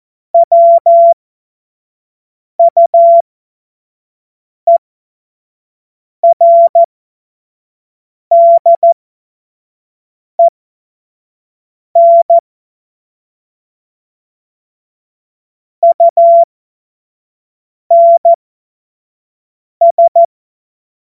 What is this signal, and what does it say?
Morse code, heard as a keyed tone, about 14 wpm, spaced as for 5 wpm, 680 Hz. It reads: WUERDEN UNS
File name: Morsekurs / Witze / Witz 08. August 2023 25 F70